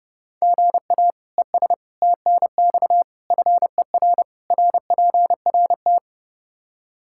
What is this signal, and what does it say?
Morse code, heard as a keyed tone, about 30 wpm, 700 Hz. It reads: GA ES TNX FER RPRT
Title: Practice QSO - Exchange 1 - Beginning - 30wpm